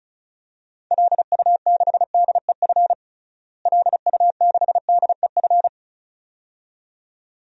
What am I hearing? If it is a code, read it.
Morse code, heard as a keyed tone, about 36 wpm, 700 Hz. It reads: LU6DEF LU6DEF
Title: DXCC-100 Call Signs - 6 Characters - Repeated 2x - 35wpm